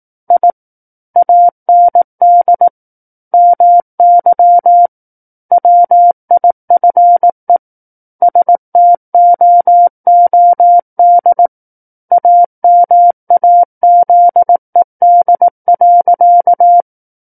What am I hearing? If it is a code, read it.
Morse code, heard as a keyed tone, about 18 wpm, 700 Hz. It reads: I AND MY WIFE STOOD AMAZED.
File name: 18 WPM